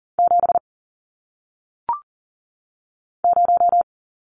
Morse code, heard as a keyed tone, about 40 wpm, 700 Hz. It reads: 7 0
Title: Numbers - 1-Digit 40wpm